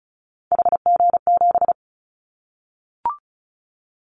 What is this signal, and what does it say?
Morse code, heard as a keyed tone, about 36 wpm, 700 Hz. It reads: HG7
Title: Callsign Prefixes for Top 50 DXCC - 35wpm